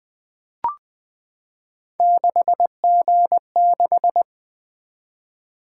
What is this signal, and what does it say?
Morse code, heard as a keyed tone, about 20 wpm, 700 Hz. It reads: 6G6
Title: Three Letter-Number Combinations - 20wpm